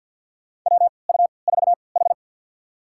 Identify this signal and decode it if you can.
Morse code, heard as a keyed tone, about 52 wpm, spaced as for 29 wpm, 700 Hz. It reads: WU4H
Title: USA Call Signs - 4 Characters - 3x Character Spacing - 50wpm